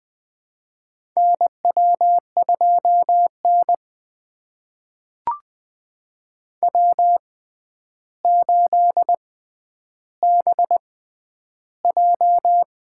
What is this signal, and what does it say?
Morse code, heard as a keyed tone, about 20 wpm, spaced as for 10 wpm, 700 Hz. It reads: NW2N W8BJ
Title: USA Call Signs - 4 Characters - 6x Character Spacing - 20wpm